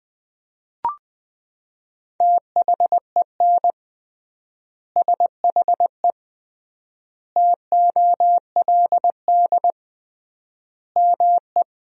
Morse code, heard as a keyed tone, about 20 wpm, 700 Hz. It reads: THEN SHE TOLD ME
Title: 3x Word-Spacing - Sentences from Top 400 Words - 20wpm